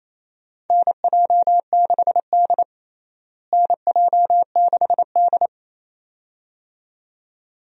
Morse code, heard as a keyed tone, about 28 wpm, 700 Hz. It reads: NJ6D NJ6D
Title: USA Call Signs - 4 Characters - Repeated 2x - 28wpm